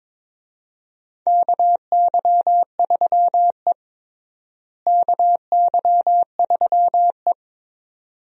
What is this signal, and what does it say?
Morse code, heard as a keyed tone, about 22 wpm, 700 Hz. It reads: KY3E KY3E
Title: USA Call Signs - 4 Characters - Repeated 2x - 22wpm